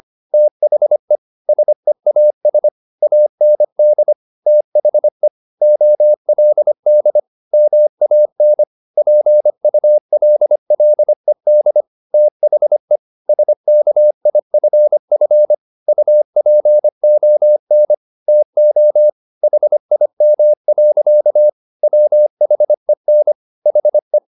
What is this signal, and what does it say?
Morse code, heard as a keyed tone, about 25 wpm, 600 Hz. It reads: THE SEAS AND THE OLD MAN PULLED THE SKIFF UPON TO HIM. WHEN HE